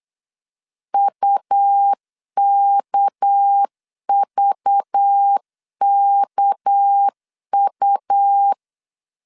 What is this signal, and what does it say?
Morse code, heard as a keyed tone, about 8 wpm, 800 Hz. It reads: UKVKU
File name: LEZIONI / 6 PKX0 / 6 PK 8wpm 3003